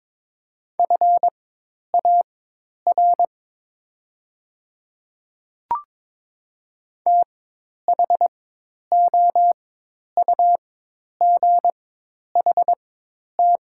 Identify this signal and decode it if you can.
Morse code, heard as a keyed tone, about 22 wpm, spaced as for 10 wpm, 700 Hz. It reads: FAR THOUGHT
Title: 4x Character Spacing - Top 500 Words - 22wpm